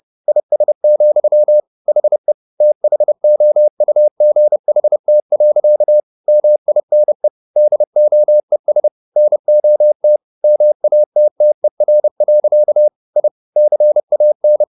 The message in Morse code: IS, HE THOUGHT. MINE DOES NOT MATTER. I CAN

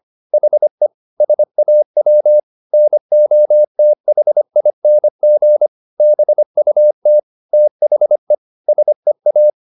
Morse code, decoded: HE SAW NOTHING BUT THE SEA

